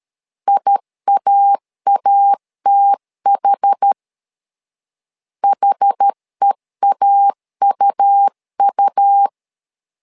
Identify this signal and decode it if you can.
Morse code, heard as a keyed tone, about 13 wpm, 800 Hz. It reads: IAATH HEAUU
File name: LEZIONI / 2 UHDN / 2 UH 12wpm 1439